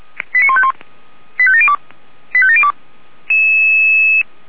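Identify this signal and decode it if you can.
Five-tone selective call (ZVEI), heard as quick runs of five tones, tones of about 70 ms. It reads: 89151 85702 85702
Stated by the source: ZVEI five-tone selective call received on SDR.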